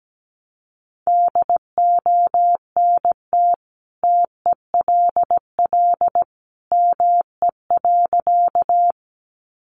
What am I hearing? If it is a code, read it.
Morse code, heard as a keyed tone, about 17 wpm, 700 Hz. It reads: DONT TELL ME.
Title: Sentences from Top 300 Words 17wpm